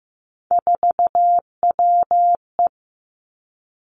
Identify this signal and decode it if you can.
Morse code, heard as a keyed tone, about 15 wpm, 700 Hz. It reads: 4WE
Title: Three Challenging Letter-Number Combinations 15wpm